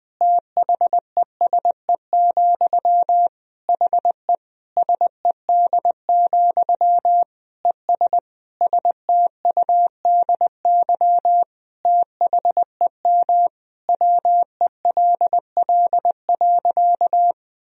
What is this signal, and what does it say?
Morse code, heard as a keyed tone, about 21 wpm, 700 Hz. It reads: THESE, HE SED, ES STUDY THEM WELL.
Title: Sentences from Top 300 Words and Abbreviations 20wpm